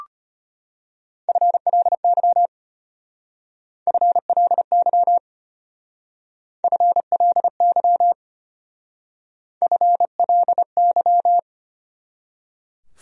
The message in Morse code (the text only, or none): FLY FLY FLY FLY